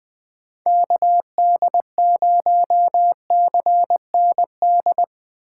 Morse code, heard as a keyed tone, about 20 wpm, 700 Hz. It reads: KD0CND
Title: USA Call Signs - 6 Characters - 3x Character Spacing - 20wpm